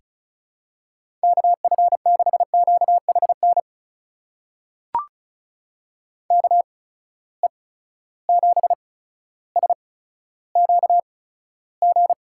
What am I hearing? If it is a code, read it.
Morse code, heard as a keyed tone, about 36 wpm, spaced as for 15 wpm, 700 Hz. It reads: KF6QHN KE7SQG
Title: USA Call Signs - 6 Characters - 8x Character Spacing - 35wpm